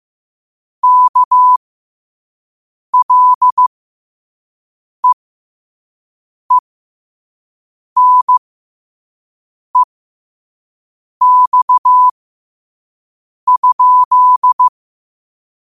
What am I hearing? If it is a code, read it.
Morse code, heard as a keyed tone, about 15 wpm, spaced as for 5 wpm, 1000 Hz. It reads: KLEENEX?
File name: qotdfiles / 5wpm